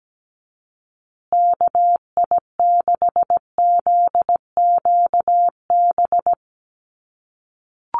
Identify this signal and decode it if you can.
Morse code, heard as a keyed tone, about 17 wpm, 700 Hz. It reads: KI6ZQB
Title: USA Call Signs Repeated 2x 17wpm